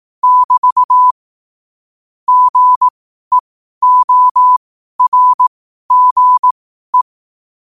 Morse code, heard as a keyed tone, about 18 wpm, spaced as for 12 wpm, 1000 Hz. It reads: <BT> GEORGE
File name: qotdfiles / 12wpm